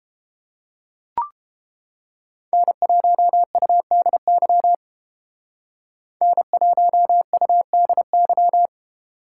Morse code, heard as a keyed tone, about 29 wpm, 700 Hz. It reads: N1UDY N1UDY